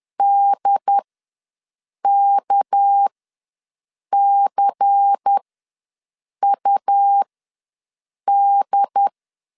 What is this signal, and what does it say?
Morse code, heard as a keyed tone, about 10 wpm, spaced as for 6 wpm, 800 Hz. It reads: DKCUD
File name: LEZIONI / 6 PKX0 / 6 PKX ws 10wpm 3706